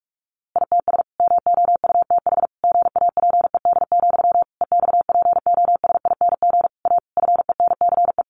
Morse code, heard as a keyed tone, about 44 wpm, 700 Hz. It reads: ITS MOUTH GAPED, EXPOSING A FENCE
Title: A Sound Of Thunder 45wpm